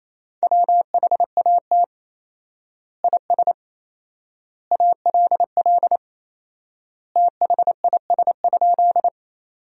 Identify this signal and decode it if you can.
Morse code, heard as a keyed tone, about 28 wpm, 700 Hz. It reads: WHAT IS ALL THIS?
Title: 4x Word-Spacing - Sentences from Top 100 Words - 28wpm